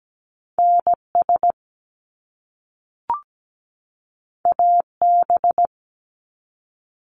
Morse code, heard as a keyed tone, about 17 wpm, 700 Hz. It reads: NS AB
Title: Canadian Provinces and Territories 17wpm